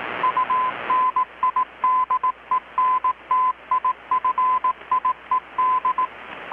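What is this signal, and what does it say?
Morse code, heard as a keyed tone, about 18 wpm, 1020 Hz. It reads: UNIDENTIFIED
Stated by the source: Morse from KPH's Night of Nights XX maritime radiotelegraphy event, received in Chiapas, Mexico about 3,800 km away on a Kenwood R-600 with a grounded 10 m long-wire antenna.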